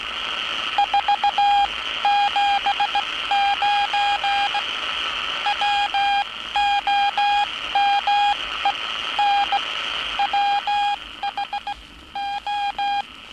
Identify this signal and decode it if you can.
Morse code, heard as a keyed tone, about 17 wpm, spaced as for 14 wpm, 810 Hz. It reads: 479 WOMEN WHO
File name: training / NAVPERS 11049to11060 / NAVPERS 11059 RA RB